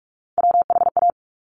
Morse code, heard as a keyed tone, about 46 wpm, 700 Hz. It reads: WHA
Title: Top Three Letter Combinations in Words 45wpm